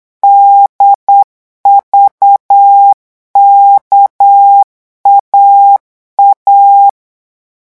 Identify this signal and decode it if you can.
Morse code, heard as a keyed tone, about 8 wpm, 800 Hz. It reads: DVKAA
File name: morse mp3 / revision3-vitesse 8 mots